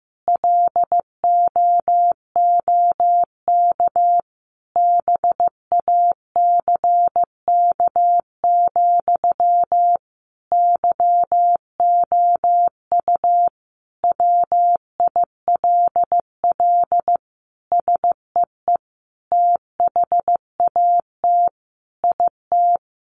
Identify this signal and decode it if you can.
Morse code, heard as a keyed tone, about 15 wpm, 700 Hz. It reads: LOOK BACK, YOU WILL SEE THAT IT
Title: Sentences from Top 800 Words 15wpm